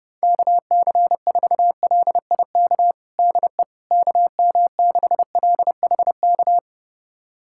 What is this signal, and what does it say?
Morse code, heard as a keyed tone, about 30 wpm, 700 Hz. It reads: KC4LIK DE KM6LHK
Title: USA Callsign DE Callsign - 6 Characters - 30wpm